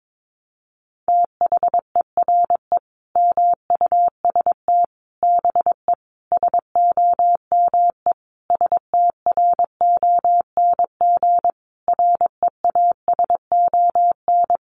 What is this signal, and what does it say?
Morse code, heard as a keyed tone, about 22 wpm, 700 Hz. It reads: THERE MUST BE SOME STRONG REASON
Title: Sentences from Top 700 Words 22wpm